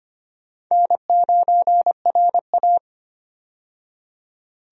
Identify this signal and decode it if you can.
Morse code, heard as a keyed tone, about 25 wpm, 700 Hz. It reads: N9RA
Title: USA Call Signs - 4 Characters - 25wpm